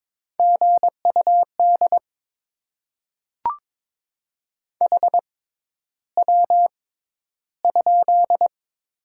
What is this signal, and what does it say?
Morse code, heard as a keyed tone, about 22 wpm, spaced as for 11 wpm, 700 Hz. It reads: GUD HW?
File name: Top 20 QSO Elements - 6x Character Spacing - 22wpm